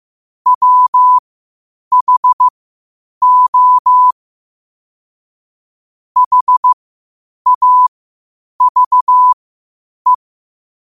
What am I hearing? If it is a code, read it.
Morse code, heard as a keyed tone, about 15 wpm, spaced as for 8 wpm, 1000 Hz. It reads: WHO HAVE
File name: 8wpm